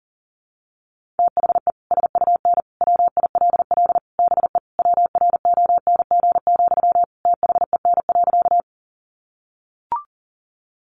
Morse code, heard as a keyed tone, about 40 wpm, 700 Hz. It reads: THE SUN WILL BE WRONG, THEN.
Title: Sentences from Top 900 Words 40wpm